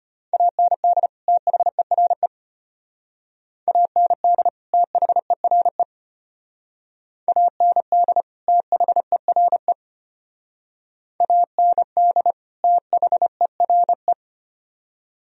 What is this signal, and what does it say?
Morse code, heard as a keyed tone, about 33 wpm, 700 Hz. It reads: AND THERE AND THERE AND THERE AND THERE